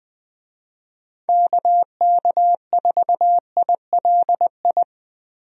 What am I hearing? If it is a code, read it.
Morse code, heard as a keyed tone, about 20 wpm, 700 Hz. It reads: KK4ILI
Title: USA Call Signs - 6 Characters - Repeated 3x - 20wpm